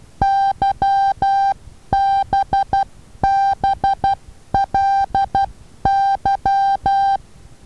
Morse code, heard as a keyed tone, about 12 wpm, 790 Hz. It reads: YBBLY
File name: Taviro HA6VA B